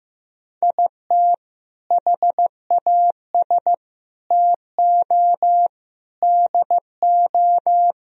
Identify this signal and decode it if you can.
Morse code, heard as a keyed tone, about 15 wpm, 700 Hz. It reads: IT HAS TO DO
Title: Speed-Racing - Sets of 5 Words 15wpm